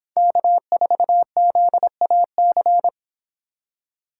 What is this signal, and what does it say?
Morse code, heard as a keyed tone, about 26 wpm, 700 Hz. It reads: K4ZAC